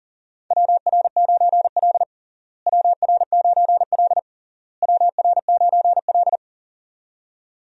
Morse code, heard as a keyed tone, about 40 wpm, 700 Hz. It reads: WR9L WR9L WR9L
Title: USA Call Signs - 4 Characters - Repeated 3x - 40wpm